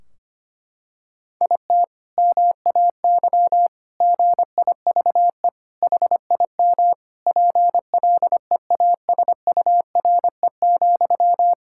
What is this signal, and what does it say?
Morse code, heard as a keyed tone, about 25 wpm, 700 Hz. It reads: IT MAY GIVE HIM PLEASURE,
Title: Sentences from Top 1600 Words 25wpm